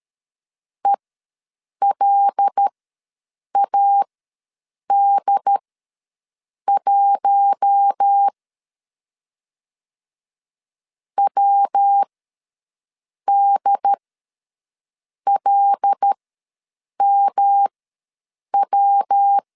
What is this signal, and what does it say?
Morse code, heard as a keyed tone, about 13 wpm, spaced as for 6 wpm, 800 Hz. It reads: ELAD1 WDLMW
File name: 5 LB ws 12wpm